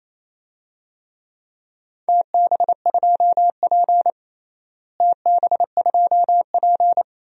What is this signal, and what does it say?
Morse code, heard as a keyed tone, about 28 wpm, 700 Hz. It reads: TB2P TB2P
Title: DXCC-100 Call Signs - 4 Characters - Repeated 2x - 28wpm